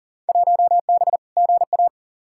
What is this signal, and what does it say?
Morse code, heard as a keyed tone, about 40 wpm, 700 Hz. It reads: 1B GA
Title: Field Day Exchanges - 40wpm